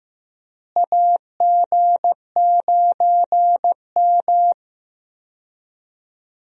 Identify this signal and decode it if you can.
Morse code, heard as a keyed tone, about 15 wpm, 700 Hz. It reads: AG9M
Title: USA Call Signs - 4 Characters - Repeated 2x - 15wpm